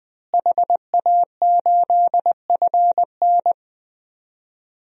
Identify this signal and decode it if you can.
Morse code, heard as a keyed tone, about 20 wpm, 700 Hz. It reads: HA8FN
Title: DXCC-100 Call Signs - 5 Characters - Repeated 3x - 20wpm